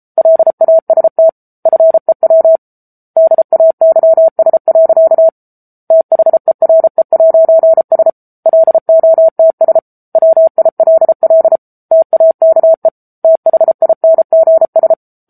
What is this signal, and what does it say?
Morse code, heard as a keyed tone, about 34 wpm, 650 Hz. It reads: LAST FEW DAYS. THERE'S LOTS WILL TAKE THINGS
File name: Book Two / 07-Chapter 7 - 33 WPM